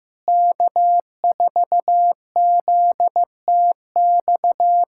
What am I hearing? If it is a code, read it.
Morse code, heard as a keyed tone, about 16 wpm, 700 Hz. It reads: K4ZTX
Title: USA Callsign DE Callsign - 5 Characters - 15wpm